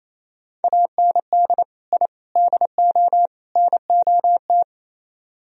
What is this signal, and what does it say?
Morse code, heard as a keyed tone, about 34 wpm, 700 Hz. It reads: AND I DO NOT
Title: Speed-Racing - Sets of 4 Words 28wpm